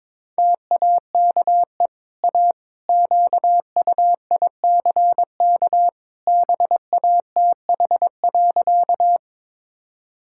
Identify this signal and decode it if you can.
Morse code, heard as a keyed tone, about 22 wpm, 700 Hz. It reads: TAKE A QUICK BATH.
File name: practice / 22wpm / Easy Sentences with Hints - 22wpm